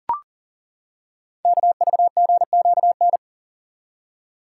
Morse code, heard as a keyed tone, about 38 wpm, 700 Hz. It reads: KVGQN
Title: Code Groups - 5 Letters - 40wpm